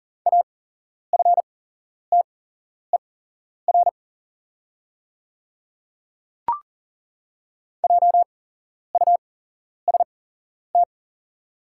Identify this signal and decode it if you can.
Morse code, heard as a keyed tone, about 40 wpm, spaced as for 11 wpm, 690 Hz. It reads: AFTER JUST
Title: 8x Character Spacing - Top 200 Words - 40wpm